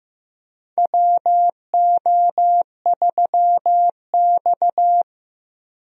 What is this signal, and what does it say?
Morse code, heard as a keyed tone, about 15 wpm, 700 Hz. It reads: WO3X
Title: USA Call Signs - 4 Characters - 15wpm